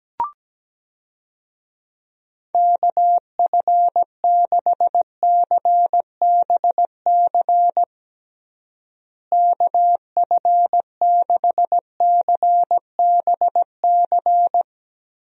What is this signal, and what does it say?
Morse code, heard as a keyed tone, about 17 wpm, 700 Hz. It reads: KF6CBC KF6CBC
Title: USA Call Signs - 6 Characters - Repeated 2x - 17wpm